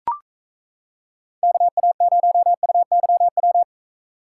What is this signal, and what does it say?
Morse code, heard as a keyed tone, about 33 wpm, 700 Hz. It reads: KA0UYW
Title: Speed Racing - USA 2x3 Call Signs 28wpm